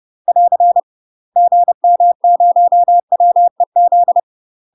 Morse code, heard as a keyed tone, about 30 wpm, 700 Hz. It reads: <AR> GM0WEZ
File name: UK MASTER QSOs TRAINING / 30WPM / UK MASTER QSOs 0031